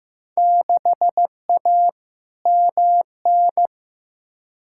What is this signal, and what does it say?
Morse code, heard as a keyed tone, about 15 wpm, 700 Hz. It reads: 6A MN
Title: Field Day Exchanges Encore - 15wpm